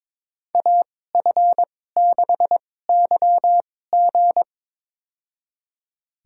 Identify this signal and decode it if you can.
Morse code, heard as a keyed tone, about 22 wpm, spaced as for 16 wpm, 700 Hz. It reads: AF6YG
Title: USA Call Signs - 5 Characters - 2x Character Spacing - 22wpm